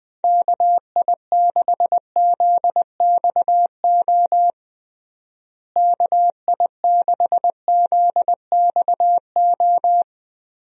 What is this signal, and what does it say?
Morse code, heard as a keyed tone, about 20 wpm, 700 Hz. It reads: KI6ZXO KI6ZXO
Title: USA Call Signs - 6 Characters - Repeated 3x - 20wpm